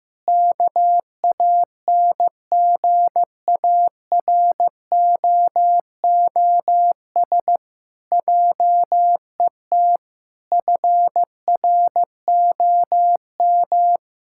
Morse code, heard as a keyed tone, about 15 wpm, 700 Hz. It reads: KANGAROOS JET FROM
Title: Pangram Challenge v2 - 15wpm